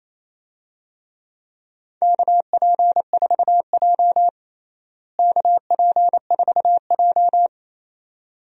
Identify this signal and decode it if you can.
Morse code, heard as a keyed tone, about 28 wpm, 700 Hz. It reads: KP4J KP4J
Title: DXCC-100 Call Signs - 4 Characters - Repeated 2x - 28wpm